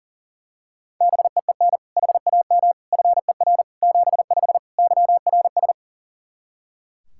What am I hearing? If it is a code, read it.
Morse code, heard as a keyed tone, about 40 wpm, 700 Hz. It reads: BEEN HAM FER 75 YRS
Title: Practice QSO - Exchange 3 - More Info - 40wpm